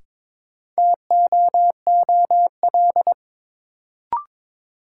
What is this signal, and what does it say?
Morse code, heard as a keyed tone, about 22 wpm, 700 Hz. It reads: TOOL
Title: Sentences from Top 1000 Words - Review Part 6 - New Word Drill - 22wpm